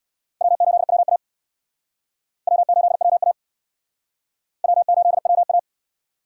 Morse code, heard as a keyed tone, about 61 wpm, 700 Hz. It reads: W8WA W8WA W8WA
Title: Speed Racing - USA Call Signs 50wpm